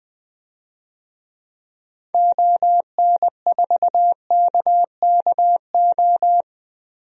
Morse code, heard as a keyed tone, about 20 wpm, 700 Hz. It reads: ON4KKO
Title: DXCC-100 Call Signs - 6 Characters - 20wpm